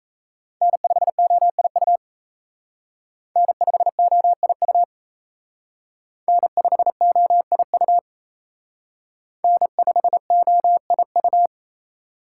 Morse code, heard as a keyed tone, about 35 wpm, 700 Hz. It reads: N5OIU N5OIU N5OIU N5OIU